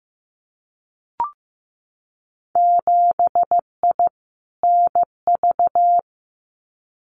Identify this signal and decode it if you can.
Morse code, heard as a keyed tone, about 15 wpm, 700 Hz. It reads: 7I NV
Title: Winter Field Day Exchanges Encore 15wpm